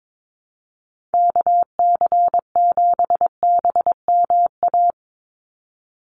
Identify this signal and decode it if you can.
Morse code, heard as a keyed tone, about 22 wpm, 700 Hz. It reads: KC7BMA
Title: USA Call Signs 22wpm